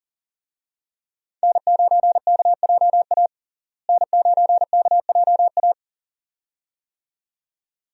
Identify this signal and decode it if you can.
Morse code, heard as a keyed tone, about 37 wpm, 700 Hz. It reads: N9KJA N9KJA
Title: USA Call Signs - 5 Characters - Repeated 2x - 40wpm